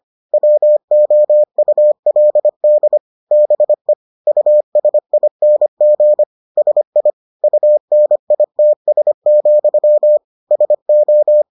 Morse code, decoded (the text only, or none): WOULD BE USING SI UNITS, SO